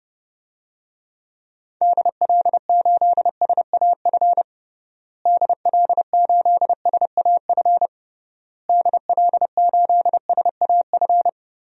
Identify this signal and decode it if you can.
Morse code, heard as a keyed tone, about 30 wpm, 700 Hz. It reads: DL8SAF DL8SAF DL8SAF
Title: DXCC-100 Call Signs - 6 Characters - Repeated 3x - 30wpm